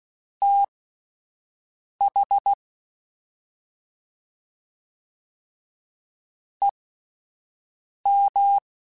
Morse code, heard as a keyed tone, about 16 wpm, spaced as for 5 wpm, 800 Hz. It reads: TH EM
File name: Misc / Misc-mp3s / Jumps @5